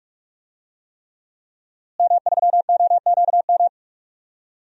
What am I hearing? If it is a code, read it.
Morse code, heard as a keyed tone, about 46 wpm, 700 Hz. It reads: M3OQM
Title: DXCC-100 Call Signs - 5 Characters - Repeated 3x - 45wpm